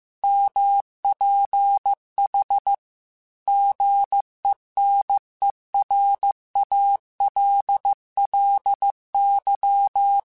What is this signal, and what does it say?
Morse code, heard as a keyed tone, about 15 wpm, 800 Hz. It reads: MPH GENERALLY
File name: Weather Sets / Weather Sets-mp3s / Weather 4 10 @13